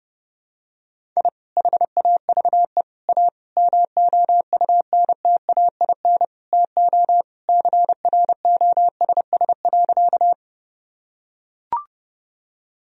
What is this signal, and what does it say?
Morse code, heard as a keyed tone, about 30 wpm, 700 Hz. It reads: I HAVE A MOUNTAIN TO CROSS.
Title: Sentences from Top 1000 Words - Review Part 1 - 30wpm